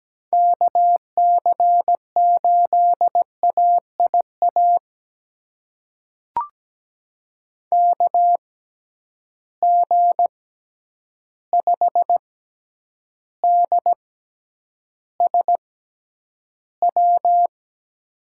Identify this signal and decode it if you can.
Morse code, heard as a keyed tone, about 17 wpm, spaced as for 9 wpm, 700 Hz. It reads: KC8AIA KG5DSW
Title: USA Call Signs - 6 Characters - 6x Character Spacing - 17wpm